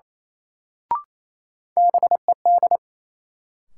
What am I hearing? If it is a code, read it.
Morse code, heard as a keyed tone, about 28 wpm, 700 Hz. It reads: BED
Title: Sentences from Top 500 Words - New Word Drill - 28wpm